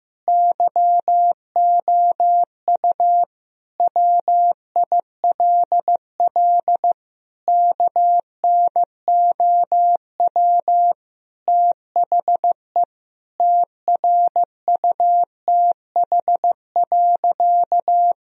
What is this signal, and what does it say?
Morse code, heard as a keyed tone, about 15 wpm, 700 Hz. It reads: YOU WILL KNOW THE TRUTH.
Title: Sentences from Top 1000 Words - Review Part 3 - 15wpm